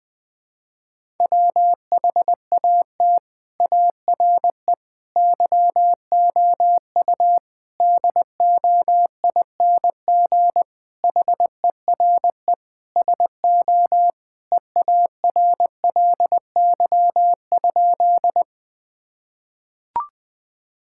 Morse code, heard as a keyed tone, about 21 wpm, 700 Hz. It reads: WHAT ARE YOU DOING HERE SO EARLY?
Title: Sentences with 5-Word Phrases - Spoken Beginning - 20wpm